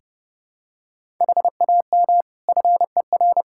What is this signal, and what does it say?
Morse code, heard as a keyed tone, about 30 wpm, 700 Hz. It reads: HAM FER
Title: Practice QSO - Exchange 3 - More Info - 30wpm